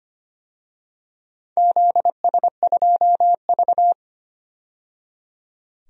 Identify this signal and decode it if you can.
Morse code, heard as a keyed tone, about 25 wpm, 700 Hz. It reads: ZS2V
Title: DXCC-100 Call Signs - 4 Characters - 25wpm